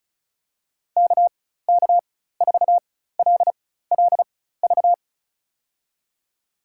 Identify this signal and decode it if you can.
Morse code, heard as a keyed tone, about 34 wpm, spaced as for 16 wpm, 700 Hz. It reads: KK4LLV
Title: USA Call Signs - 6 Characters - 4x Character Spacing - 35wpm